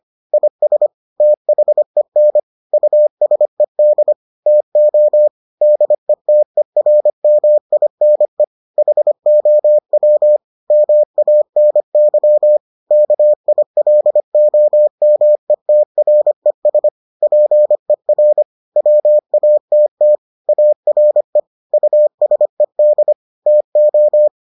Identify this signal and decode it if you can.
Morse code, heard as a keyed tone, about 25 wpm, 600 Hz. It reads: IS THEN USED TO DETERMINE HOW MANY KILOMETRES PER WATT ARE USED TO